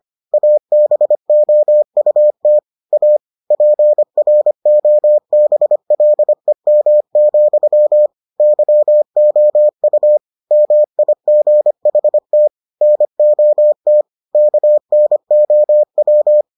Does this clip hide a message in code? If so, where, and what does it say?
Yes: Morse code from 0.3 to 16.4 s, heard as a keyed tone, about 25 wpm, 600 Hz. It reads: ABOUT A PROBLEM, YOU MIGHT NOT KNOW